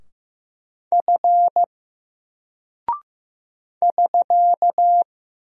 Morse code, heard as a keyed tone, about 15 wpm, 700 Hz. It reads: F <SK>
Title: Single Character (Letter:Number:Punctuation:Prosign) 15wpm